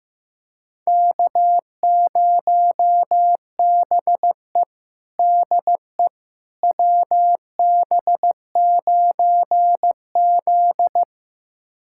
Morse code, heard as a keyed tone, about 15 wpm, 700 Hz. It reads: K0BE DE WB9Z
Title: USA Callsign DE Callsign - 4 Characters - 15wpm